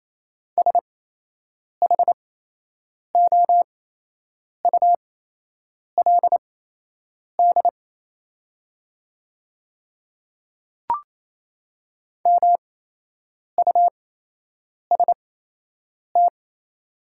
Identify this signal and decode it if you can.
Morse code, heard as a keyed tone, about 27 wpm, spaced as for 8 wpm, 700 Hz. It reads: SHOULD MUST
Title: 8x Character Spacing - Top 300 Words - 28wpm